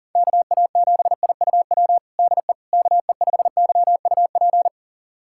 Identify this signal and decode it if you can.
Morse code, heard as a keyed tone, about 40 wpm, 700 Hz. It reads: KA7IUW DE KE5YUP
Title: USA Callsign DE Callsign - 6 Characters - 40wpm